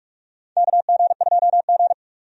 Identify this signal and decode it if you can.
Morse code, heard as a keyed tone, about 41 wpm, 700 Hz. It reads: KG2G
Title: USA Call Signs - 4 Characters - 45wpm